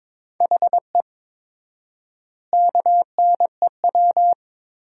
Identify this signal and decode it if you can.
Morse code, heard as a keyed tone, about 22 wpm, 700 Hz. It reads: HE KNEW
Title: 4x Word-Spacing - Sentences from Top 500 Words - 22wpm